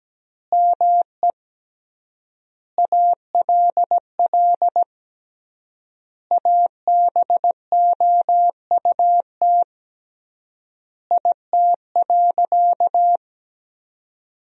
Morse code, heard as a keyed tone, about 17 wpm, 700 Hz. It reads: ME ALL ABOUT IT.
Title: 3x Word-Spacing - Sentences from Top 200 Words - 17wpm